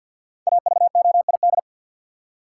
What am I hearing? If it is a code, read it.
Morse code, heard as a keyed tone, about 53 wpm, 700 Hz. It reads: AVOID
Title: Five Letter Words - 50wpm